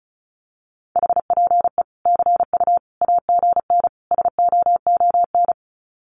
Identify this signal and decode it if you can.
Morse code, heard as a keyed tone, about 34 wpm, 700 Hz. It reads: HPE CU AGN SOON
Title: Common QSO Phrases 35wpm